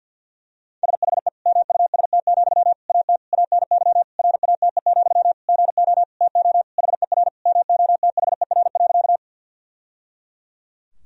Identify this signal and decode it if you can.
Morse code, heard as a keyed tone, about 48 wpm, 700 Hz. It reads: SHE MUST, AT ANY RATE, GO TO HER MOTHER.
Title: Sentences from Top 1000 Words - Review Part 4 - 50wpm